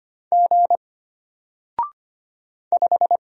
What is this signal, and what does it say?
Morse code, heard as a keyed tone, about 25 wpm, 700 Hz. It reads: G 5
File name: Single-Letter-Number-25wpm